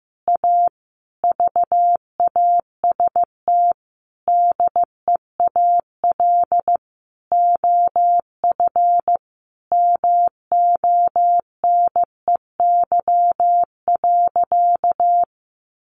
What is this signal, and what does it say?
Morse code, heard as a keyed tone, about 15 wpm, 700 Hz. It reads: A VAST DEAL OF MONEY.
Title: Sentences from Top 1300 Words 15wpm